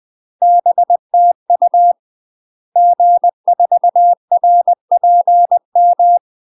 Morse code, heard as a keyed tone, about 20 wpm, 700 Hz. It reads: BTU G4RPM
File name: UK MASTER QSOs TRAINING / 20WPM / UK MASTER QSOs 0030